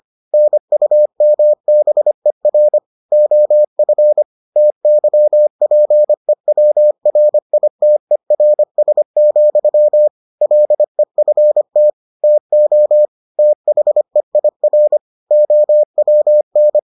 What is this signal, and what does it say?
Morse code, heard as a keyed tone, about 25 wpm, 600 Hz. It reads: NUMBER OF TYPEWRITERS, LEFT TO THEIR OWN